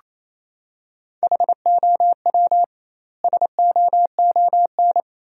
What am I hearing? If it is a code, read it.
Morse code, heard as a keyed tone, about 28 wpm, 700 Hz. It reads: HOW SOON
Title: 2x Word-Spacing - Sentences from Top 400 Words - 28wpm